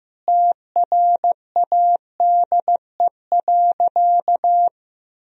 Morse code, heard as a keyed tone, about 15 wpm, 700 Hz. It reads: TRADE.
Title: Sentences from Top 1000 Words - Review Part 4 - 15wpm